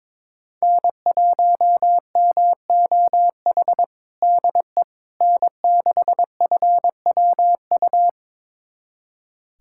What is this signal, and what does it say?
Morse code, heard as a keyed tone, about 22 wpm, 700 Hz. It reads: N1MOH DE N6FWU
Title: USA Callsign DE Callsign - 5 Characters - 22wpm